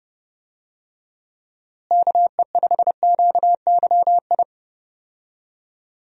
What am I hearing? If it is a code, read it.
Morse code, heard as a keyed tone, about 30 wpm, 700 Hz. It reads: KE5QYI
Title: USA Call Signs - 6 Characters - Repeated 2x - 30wpm